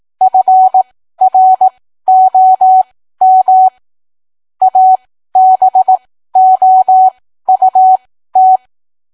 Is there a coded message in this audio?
Yes: Morse code from 0.2 to 8.6 s, heard as a keyed tone, about 18 wpm, spaced as for 13 wpm, 750 Hz. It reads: FROM ABOUT